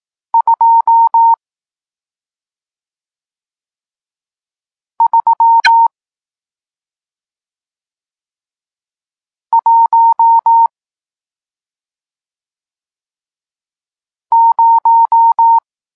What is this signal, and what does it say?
Morse code, heard as a keyed tone, about 18 wpm, 920 Hz. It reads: 2 3 1 0
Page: Tekst 40 – Nauka znaków przestankowych